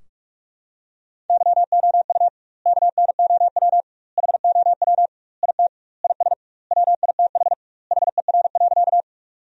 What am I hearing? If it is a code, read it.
Morse code, heard as a keyed tone, about 46 wpm, 700 Hz. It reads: YOU KNOW HOW IT IS WITH HER.
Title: 2x Word-Spacing - Sentences from Top 100 Words - 45wpm